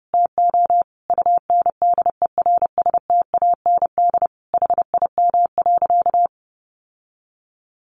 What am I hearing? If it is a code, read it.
Morse code, heard as a keyed tone, about 30 wpm, 700 Hz. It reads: TO UNDERSTAND HIM.
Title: Sentences from Top 600 Words 30wpm